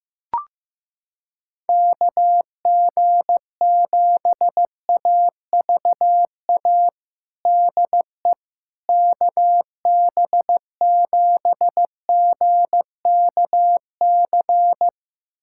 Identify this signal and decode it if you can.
Morse code, heard as a keyed tone, about 16 wpm, 700 Hz. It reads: KG7AVA DE KB7GKC
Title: USA Callsign DE Callsign - 6 Characters - 15wpm